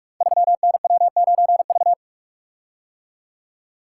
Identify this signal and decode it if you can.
Morse code, heard as a keyed tone, about 44 wpm, 700 Hz. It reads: 3NW9V
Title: Code Groups - 5 Letter:Numbers - 45wpm